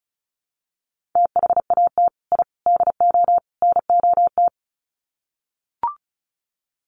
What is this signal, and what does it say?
Morse code, heard as a keyed tone, about 34 wpm, 700 Hz. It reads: THAT I DO NOT
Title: Sets of 4 Words 35wpm